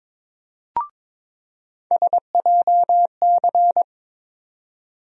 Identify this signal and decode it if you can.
Morse code, heard as a keyed tone, about 22 wpm, 700 Hz. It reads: SJC
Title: USA Call Sign Postfixes - 3 Letters 22wpm